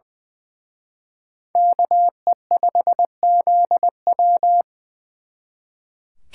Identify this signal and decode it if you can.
Morse code, heard as a keyed tone, about 24 wpm, 700 Hz. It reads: KE5ZW
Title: Speed Racing - USA 2x2 Call Signs 20wpm